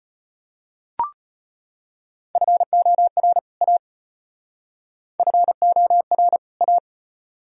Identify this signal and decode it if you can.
Morse code, heard as a keyed tone, about 31 wpm, 700 Hz. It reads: FOR A FOR A
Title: Speed-Racing - Sets of 2 Words 25wpm